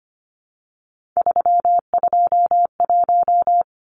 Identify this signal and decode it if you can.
Morse code, heard as a keyed tone, about 25 wpm, 700 Hz. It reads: 321
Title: Numbers - 3-Digit 25wpm